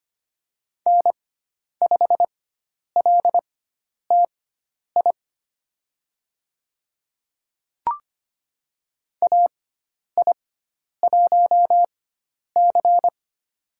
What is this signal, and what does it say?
Morse code, heard as a keyed tone, about 25 wpm, spaced as for 10 wpm, 700 Hz. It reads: N5LTI AI1C
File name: Lesson 16g - Letter c - Call Signs 25wpm